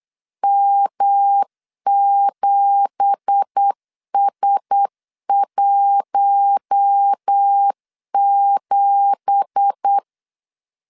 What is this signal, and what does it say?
Morse code, heard as a keyed tone, about 8 wpm, 800 Hz. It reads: M7S17
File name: 8wpm